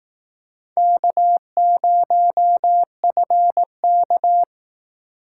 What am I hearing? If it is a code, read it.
Morse code, heard as a keyed tone, about 19 wpm, 700 Hz. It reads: K0FK